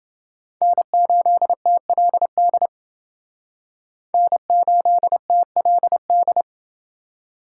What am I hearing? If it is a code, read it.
Morse code, heard as a keyed tone, about 28 wpm, 700 Hz. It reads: N8TLD N8TLD